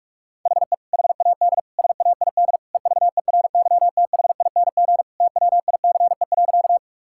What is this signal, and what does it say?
Morse code, heard as a keyed tone, about 44 wpm, 700 Hz. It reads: HE HAD SAID EVERYTHING TWICE.
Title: Sentences from Top 1600 Words 45wpm